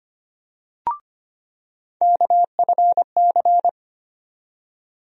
Morse code, heard as a keyed tone, about 25 wpm, 700 Hz. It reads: KFC
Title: USA Call Sign Postfixes - 3 Letters 25wpm